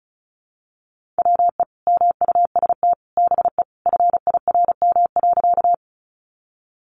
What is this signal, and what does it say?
Morse code, heard as a keyed tone, about 34 wpm, 700 Hz. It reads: WE MUST BE FIRM.
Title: Sentences from Top 1400 Words 35wpm